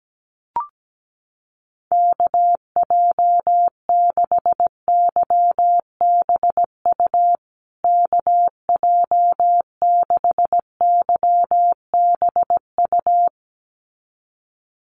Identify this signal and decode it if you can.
Morse code, heard as a keyed tone, about 17 wpm, 700 Hz. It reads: KJ6YBU KJ6YBU
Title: USA Call Signs Repeated 2x 17wpm